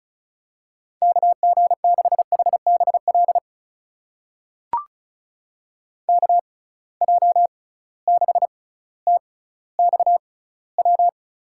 Morse code, heard as a keyed tone, about 36 wpm, spaced as for 18 wpm, 700 Hz. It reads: KG6HBL KJ6TXW
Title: USA Call Signs - 6 Characters - 6x Character Spacing - 35wpm